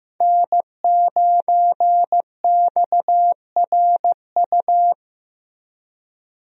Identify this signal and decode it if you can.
Morse code, heard as a keyed tone, about 15 wpm, 700 Hz. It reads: N9XRU
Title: USA Call Signs - 5 Characters - Repeated 3x - 15wpm